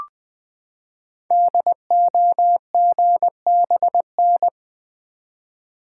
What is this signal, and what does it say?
Morse code, heard as a keyed tone, about 20 wpm, 700 Hz. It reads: DOGBN
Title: Code Groups - 5 Letters - 20wpm